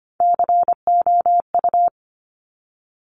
Morse code, heard as a keyed tone, about 25 wpm, 700 Hz. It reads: COU
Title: Top Three Letter Combinations in Words 25wpm